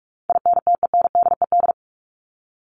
Morse code, heard as a keyed tone, about 44 wpm, 700 Hz. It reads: INTENDED
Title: ED Suffix - Top 100 Words with Base Words 45wpm